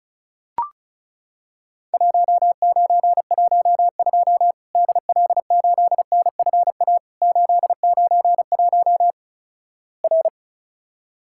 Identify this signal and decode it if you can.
Morse code, heard as a keyed tone, about 36 wpm, 700 Hz. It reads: 1912 DL8NFA 891 R
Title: Worked All Europe DX Contest - QTC Exchange - 35wpm